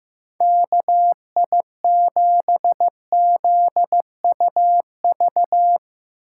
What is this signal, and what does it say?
Morse code, heard as a keyed tone, about 16 wpm, 700 Hz. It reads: KI7ZUV
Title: Speed Racing - USA 2x3 Call Signs 15wpm